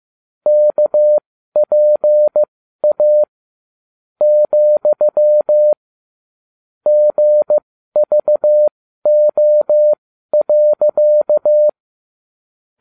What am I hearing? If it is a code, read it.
Morse code, heard as a keyed tone, about 15 wpm, spaced as for 12 wpm, 600 Hz. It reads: KPA , GVO.